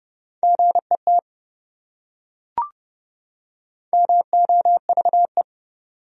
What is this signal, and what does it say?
Morse code, heard as a keyed tone, about 28 wpm, 700 Hz. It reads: GET MOVE